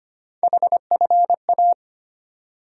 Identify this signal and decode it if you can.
Morse code, heard as a keyed tone, about 25 wpm, 700 Hz. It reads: HFA
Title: USA Call Sign Postfixes - 3 Letters 25wpm